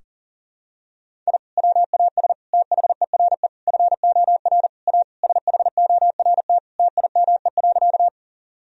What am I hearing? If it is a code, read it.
Morse code, heard as a keyed tone, about 40 wpm, 700 Hz. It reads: I WAS THERE FOR A SHORT TIME.
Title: Sentences from Top 1000 Words - Review Part 2 - 40wpm